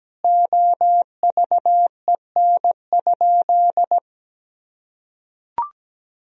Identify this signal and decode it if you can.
Morse code, heard as a keyed tone, about 17 wpm, 700 Hz. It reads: OVEN?
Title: Sentences from Top 1000 Words - Review Part 4 - 17wpm